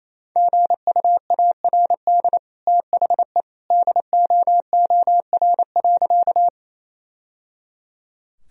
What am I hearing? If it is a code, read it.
Morse code, heard as a keyed tone, about 28 wpm, 700 Hz. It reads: GUARD THE DOOR.
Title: Sentences from Top 1000 Words - Review Part 4 - 28wpm